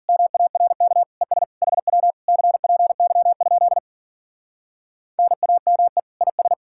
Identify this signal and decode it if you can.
Morse code, heard as a keyed tone, about 49 wpm, 700 Hz. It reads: MARK ES HW CPY? NAME IS
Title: Speed-Racing Common QSO Phrases Encore - 40wpm